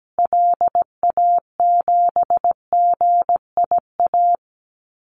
Morse code, heard as a keyed tone, about 17 wpm, 700 Hz. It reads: LA7GIA
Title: Call Signs Repeated 2x - 17wpm